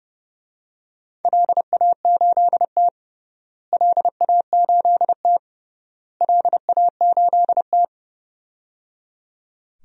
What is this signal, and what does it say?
Morse code, heard as a keyed tone, about 30 wpm, 700 Hz. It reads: LA8T LA8T LA8T
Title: DXCC-100 Call Signs - 4 Characters - Repeated 3x - 30wpm